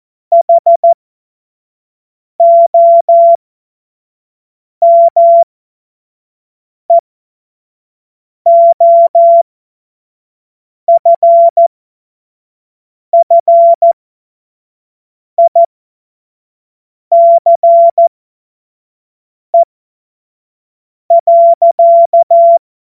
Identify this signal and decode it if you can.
Morse code, heard as a keyed tone, about 14 wpm, spaced as for 5 wpm, 680 Hz. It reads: HOMEOFFICE.
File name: Morsekurs / Witze / Witz 20. Juli 2021, 25 F70